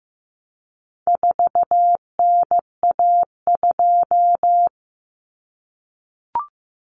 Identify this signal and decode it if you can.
Morse code, heard as a keyed tone, about 15 wpm, 700 Hz. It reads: 4NA2
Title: Common Cut Numbers - 4-Digit 15wpm